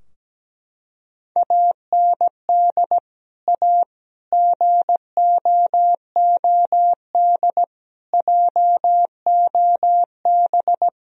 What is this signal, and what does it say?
Morse code, heard as a keyed tone, about 17 wpm, 700 Hz. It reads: AND A GOOD JOB
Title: Sentences from Top 1000 Words - Review Part 3 - 17wpm